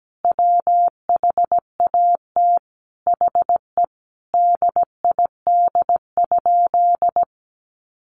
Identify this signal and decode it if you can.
Morse code, heard as a keyed tone, about 17 wpm, 700 Hz. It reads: WHAT HE DID?
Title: Sentences from Top 100 Words 17wpm